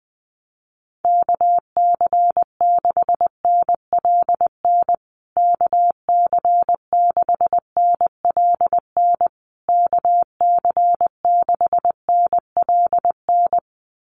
Morse code, heard as a keyed tone, about 20 wpm, 700 Hz. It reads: KC6NLN KC6NLN KC6NLN
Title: USA Call Signs Repeated 3x 20wpm